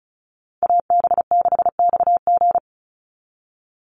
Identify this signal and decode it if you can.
Morse code, heard as a keyed tone, about 34 wpm, 700 Hz. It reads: AB6XG
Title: Call Signs Repeated 2x - 35wpm